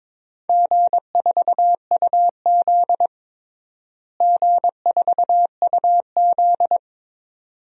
Morse code, heard as a keyed tone, about 22 wpm, 700 Hz. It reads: G4UZ G4UZ
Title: DXCC-100 Call Signs - 4 Characters - Repeated 2x - 22wpm